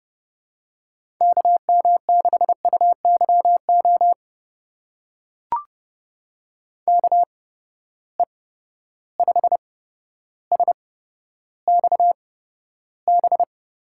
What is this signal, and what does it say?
Morse code, heard as a keyed tone, about 30 wpm, spaced as for 13 wpm, 700 Hz. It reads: KM6UYO KE5SXB
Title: USA Call Signs - 6 Characters - 8x Character Spacing - 30wpm